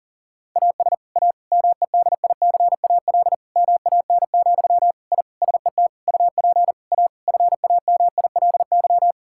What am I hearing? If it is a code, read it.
Morse code, heard as a keyed tone, about 40 wpm, 700 Hz. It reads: AS A MEDICAL MAN, I SET UP A FAMILY
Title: Sentences from Top 1000 Words - Review Part 5 - 40wpm